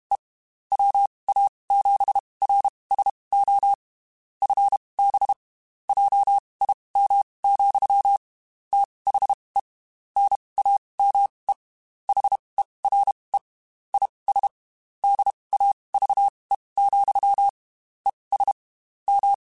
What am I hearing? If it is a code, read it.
Morse code, heard as a keyed tone, about 32 wpm, spaced as for 23 wpm, 800 Hz. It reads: E WA7RSO FB JIM, THE NAME HERE IS DAVE, ES M